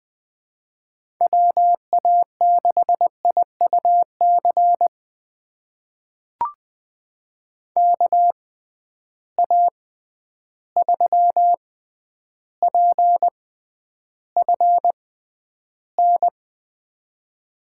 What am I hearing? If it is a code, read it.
Morse code, heard as a keyed tone, about 20 wpm, spaced as for 10 wpm, 700 Hz. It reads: WA6IUC KA3PFN
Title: USA Call Signs - 6 Characters - 6x Character Spacing - 20wpm